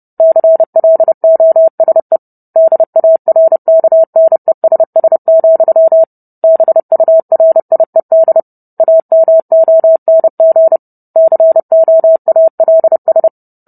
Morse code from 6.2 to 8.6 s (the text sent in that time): BURIED